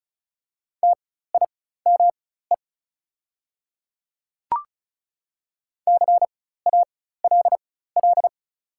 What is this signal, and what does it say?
Morse code, heard as a keyed tone, about 36 wpm, spaced as for 16 wpm, 690 Hz. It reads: TIME CALL
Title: 4x Character Spacing - Top 200 Words - 35wpm